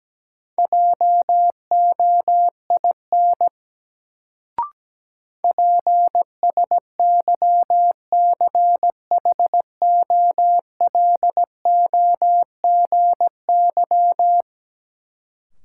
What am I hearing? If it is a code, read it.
Morse code, heard as a keyed tone, about 17 wpm, 700 Hz. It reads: JOIN PSYCHOLOGY
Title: Sentences from Top 1400 Words - New Word Drill Alt - 17wpm